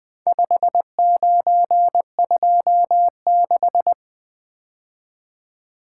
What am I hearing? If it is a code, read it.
Morse code, heard as a keyed tone, about 20 wpm, 700 Hz. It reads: 5926
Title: Four Letter-Number Combinations - 20wpm